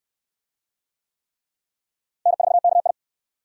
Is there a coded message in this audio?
Yes: Morse code from 2.3 to 2.9 s, heard as a keyed tone, about 68 wpm, 700 Hz. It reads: N4KI